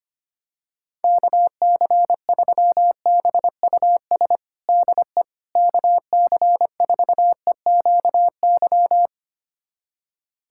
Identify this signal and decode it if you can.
Morse code, heard as a keyed tone, about 25 wpm, 700 Hz. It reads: KC3BUS DE KC4EQY
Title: USA Callsign DE Callsign 25wpm